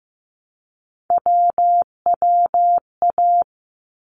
Morse code, heard as a keyed tone, about 15 wpm, 700 Hz. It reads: WWA
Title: ARRL Sections 15wpm